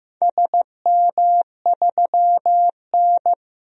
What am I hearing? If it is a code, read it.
Morse code, heard as a keyed tone, about 15 wpm, 700 Hz. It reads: SM3N
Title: DXCC-100 Call Signs - 4 Characters - 15wpm